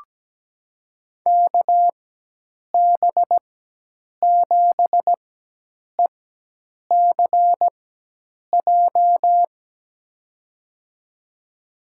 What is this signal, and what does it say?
Morse code, heard as a keyed tone, about 17 wpm, spaced as for 8 wpm, 700 Hz. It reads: KB7ECJ
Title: USA Call Signs - 4x Character Spacing - 17wpm